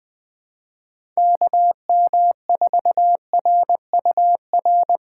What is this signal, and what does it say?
Morse code, heard as a keyed tone, about 20 wpm, 700 Hz. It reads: KM4RUR
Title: USA Call Signs - 6 Characters - 3x Character Spacing - 20wpm